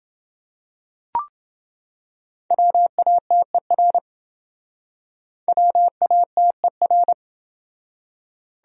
Morse code, heard as a keyed tone, about 27 wpm, 700 Hz. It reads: WATER WATER